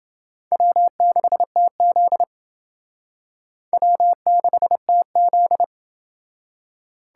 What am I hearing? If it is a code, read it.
Morse code, heard as a keyed tone, about 28 wpm, 700 Hz. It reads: W6TZ W6TZ